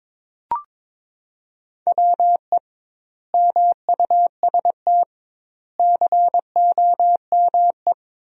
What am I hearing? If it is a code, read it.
Morse code, heard as a keyed tone, about 22 wpm, 700 Hz. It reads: WE MUST COME
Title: 2x Word-Spacing - Sentences from Top 200 Words - 22wpm